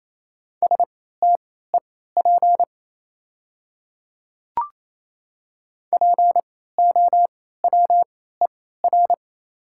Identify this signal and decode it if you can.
Morse code, heard as a keyed tone, about 28 wpm, spaced as for 16 wpm, 690 Hz. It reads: STEP POWER
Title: 3x Character Spacing - Top 500 Words - 28wpm